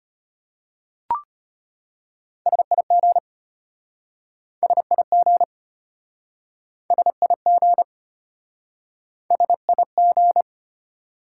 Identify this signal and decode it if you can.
Morse code, heard as a keyed tone, about 33 wpm, 700 Hz. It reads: SIG SIG SIG SIG